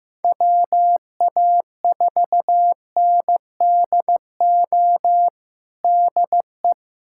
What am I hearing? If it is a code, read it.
Morse code, heard as a keyed tone, about 15 wpm, 700 Hz. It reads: WA4NDO DE
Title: USA Callsign DE Callsign 15wpm